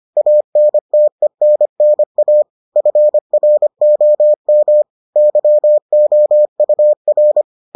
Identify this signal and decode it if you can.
Morse code, heard as a keyed tone, about 25 wpm, 600 Hz. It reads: ANTENNA FROM YOUR